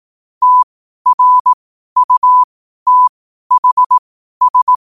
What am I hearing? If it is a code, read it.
Morse code, heard as a keyed tone, about 18 wpm, spaced as for 12 wpm, 1000 Hz. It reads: TRUTHS
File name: qotdfiles / 12wpm